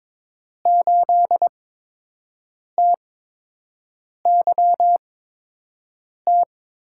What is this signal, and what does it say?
Morse code, heard as a keyed tone, about 22 wpm, spaced as for 6 wpm, 700 Hz. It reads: 8TYT
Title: USA Call Signs - 5 Characters - 8x Character Spacing - 22wpm